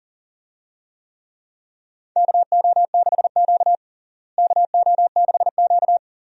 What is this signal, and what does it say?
Morse code, heard as a keyed tone, about 37 wpm, 700 Hz. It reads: KO6Q KO6Q
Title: USA Call Signs - 4 Characters - Repeated 2x - 40wpm